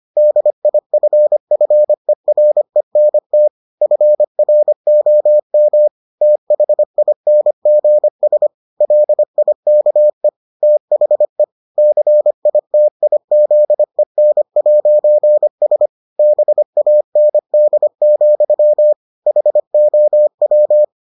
Morse code, heard as a keyed tone, about 25 wpm, 600 Hz. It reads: DIFFERENT FROM THINGS LIKE THE CITIZEN'S BAND, HOW